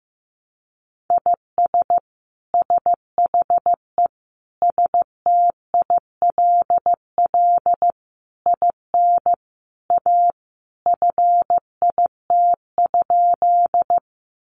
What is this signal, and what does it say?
Morse code, heard as a keyed tone, about 15 wpm, 700 Hz. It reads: IS SHE STILL IN A FIT?
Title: Sentences from Top 800 Words 15wpm